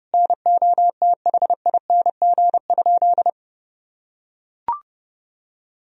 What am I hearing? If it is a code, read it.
Morse code, heard as a keyed tone, about 30 wpm, 700 Hz. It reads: NOTHING?
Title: Sentences from Top 500 Words and Abbreviations 30wpm